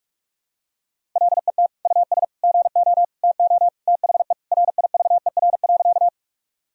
Morse code, heard as a keyed tone, about 41 wpm, 700 Hz. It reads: LET US GO TO THE RIVER.
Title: Sentences from Top 1000 Words - Review Part 1 - 45wpm